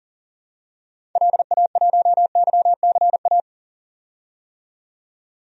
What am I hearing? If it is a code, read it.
Morse code, heard as a keyed tone, about 40 wpm, 700 Hz. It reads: LA1YCA
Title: DXCC-100 Call Signs - 6 Characters - 40wpm